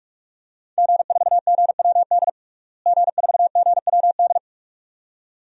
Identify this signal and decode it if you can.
Morse code, heard as a keyed tone, about 41 wpm, 700 Hz. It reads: G4GWD G4GWD
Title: DXCC-100 Call Signs - 5 Characters - Repeated 2x - 45wpm